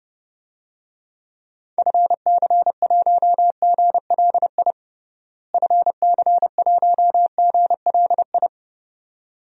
Morse code, heard as a keyed tone, about 30 wpm, 700 Hz. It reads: FC1GLI FC1GLI
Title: DXCC-100 Call Signs - 6 Characters - Repeated 2x - 30wpm